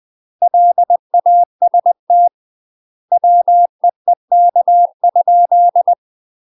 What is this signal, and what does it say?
Morse code, heard as a keyed tone, about 20 wpm, 700 Hz. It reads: LAST WEEK?
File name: UK MASTER QSOs TRAINING / 20WPM / UK MASTER QSOs 0052